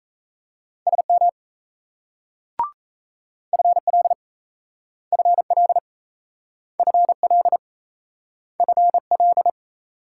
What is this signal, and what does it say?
Morse code, heard as a keyed tone, about 37 wpm, 700 Hz. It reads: SM FL FL FL FL